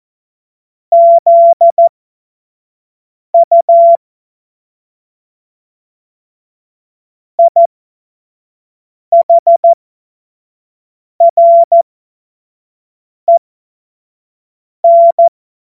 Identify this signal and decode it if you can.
Morse code, heard as a keyed tone, about 14 wpm, spaced as for 5 wpm, 680 Hz. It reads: ZU IHREN
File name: Morsekurs / Witze / Witz 28. Dezember 2021, 25 F70